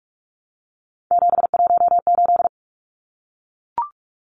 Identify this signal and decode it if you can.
Morse code, heard as a keyed tone, about 44 wpm, 700 Hz. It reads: 718
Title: Numbers - 3-Digit 45wpm